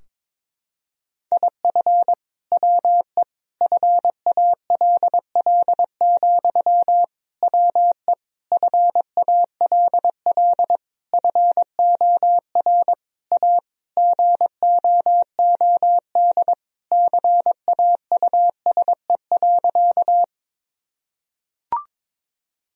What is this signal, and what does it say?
Morse code, heard as a keyed tone, about 22 wpm, 700 Hz. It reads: IF WE FALL, WE FALL FOR A GOOD CAUSE.
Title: Sentences from Top 1000 Words - Review Part 2 - 22wpm